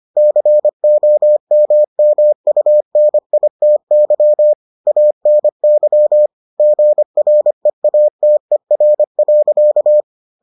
Morse code, heard as a keyed tone, about 25 wpm, 600 Hz. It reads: COMMUNITY ANY GREATER.